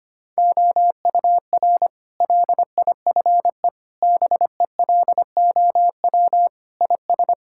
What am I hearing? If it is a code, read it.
Morse code, heard as a keyed tone, about 25 wpm, 700 Hz. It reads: OUR LIFE BELOW IS
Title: Sentences from Top 1000 Words - Review Part 5 - 25wpm